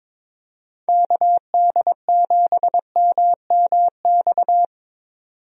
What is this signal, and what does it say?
Morse code, heard as a keyed tone, about 22 wpm, 700 Hz. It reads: KD7MMX
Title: USA Call Signs - 6 Characters - 22wpm